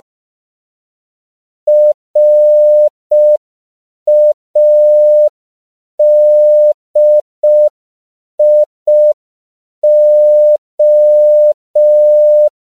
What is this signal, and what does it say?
Morse code, heard as a keyed tone, about 5 wpm, 600 Hz. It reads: RADIO
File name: CW / 5PPM